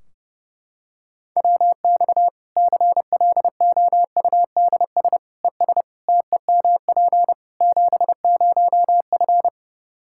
Morse code, heard as a keyed tone, about 32 wpm, 700 Hz. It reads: WX CLOUDS ES TEMP 70F
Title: Practice QSO - Exchange 2 - Info - 30wpm